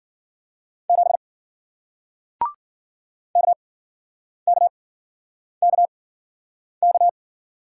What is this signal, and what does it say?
Morse code, heard as a keyed tone, about 53 wpm, spaced as for 9 wpm, 700 Hz. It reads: 7 KKKK